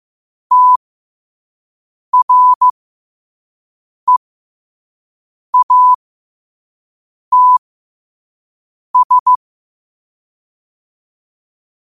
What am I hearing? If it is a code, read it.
Morse code, heard as a keyed tone, about 15 wpm, spaced as for 5 wpm, 1000 Hz. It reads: TREATS